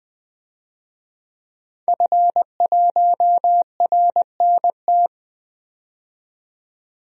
Morse code, heard as a keyed tone, about 20 wpm, 700 Hz. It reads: F1RNT
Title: DXCC-100 Call Signs - 5 Characters - 20wpm